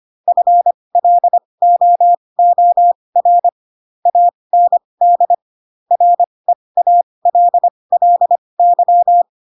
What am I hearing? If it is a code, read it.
Morse code, heard as a keyed tone, about 25 wpm, spaced as for 20 wpm, 700 Hz. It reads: FLOOR AND REALLY
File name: Pinocchio / Pinocchio ch7 15 25 ik1hge